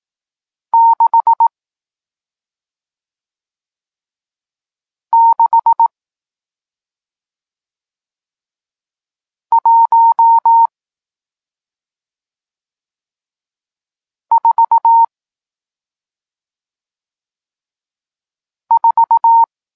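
Morse code, heard as a keyed tone, about 18 wpm, spaced as for 2 wpm, 920 Hz. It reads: 66144
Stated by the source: Morse code lesson teaching digits 1, 4 and 6, from SP9MOA, Niepołomicki Klub Krótkofalowców (Poland) amateur radio club.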